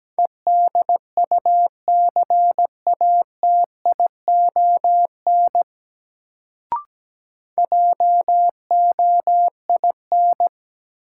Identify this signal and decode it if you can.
Morse code, heard as a keyed tone, about 17 wpm, 700 Hz. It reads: EDUCATION JOIN
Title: Sentences from Top 1000 Words - Review Part 6 - New Word Drill - 17wpm